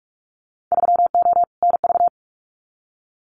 Morse code, heard as a keyed tone, about 46 wpm, 700 Hz. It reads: 3O NV
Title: Winter Field Day Exchanges Encore 45wpm